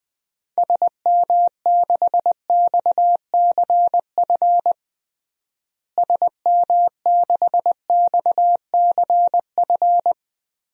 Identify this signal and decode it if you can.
Morse code, heard as a keyed tone, about 20 wpm, 700 Hz. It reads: SM6XCF SM6XCF
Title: DXCC-100 Call Signs - 6 Characters - Repeated 3x - 20wpm